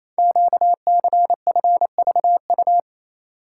Morse code, heard as a keyed tone, about 27 wpm, 700 Hz. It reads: QCFVU
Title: Code Groups - 5 Letters - 28wpm